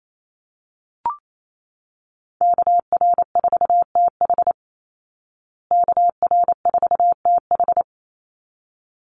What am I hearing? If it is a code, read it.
Morse code, heard as a keyed tone, about 28 wpm, 690 Hz. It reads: KR4TH KR4TH
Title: Call Signs Repeated 3x - 28wpm